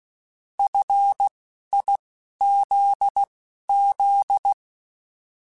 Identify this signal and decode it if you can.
Morse code, heard as a keyed tone, about 16 wpm, spaced as for 11 wpm, 800 Hz. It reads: FIZZ
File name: Alphabetical / Alphabetical mp3s / Fs @10